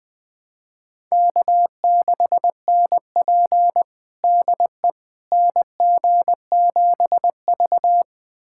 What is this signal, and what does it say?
Morse code, heard as a keyed tone, about 20 wpm, 700 Hz. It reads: K6NP DE NG7V
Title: USA Callsign DE Callsign - 4 Characters - 20wpm